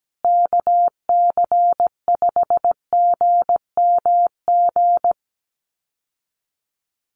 Morse code, heard as a keyed tone, about 17 wpm, 700 Hz. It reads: KC5GMG
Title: USA Call Signs Repeated 3x 17wpm